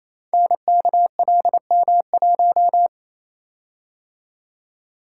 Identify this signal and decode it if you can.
Morse code, heard as a keyed tone, about 28 wpm, 700 Hz. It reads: NKLM1
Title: Code Groups - 5 Letter:Numbers - 28wpm